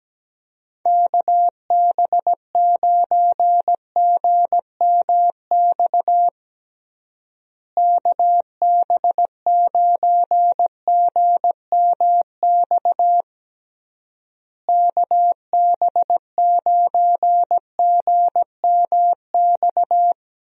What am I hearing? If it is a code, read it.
Morse code, heard as a keyed tone, about 17 wpm, 700 Hz. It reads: KB9GMX KB9GMX KB9GMX
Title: USA Call Signs - 6 Characters - Repeated 3x - 17wpm